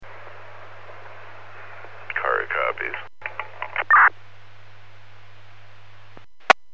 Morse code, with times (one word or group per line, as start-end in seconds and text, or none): none